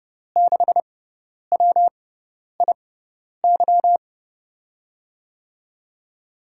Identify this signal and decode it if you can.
Morse code, heard as a keyed tone, about 31 wpm, spaced as for 10 wpm, 700 Hz. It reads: 6WIY
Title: USA Call Signs - 5 Characters - 6x Character Spacing - 30wpm